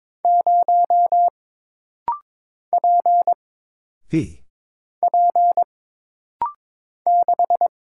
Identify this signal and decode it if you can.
Morse code, heard as a keyed tone, about 22 wpm, 700 Hz. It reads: 0 P P 6
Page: This rapid-fire timing is geared to push your ICR (Instant Character Recognition) skills.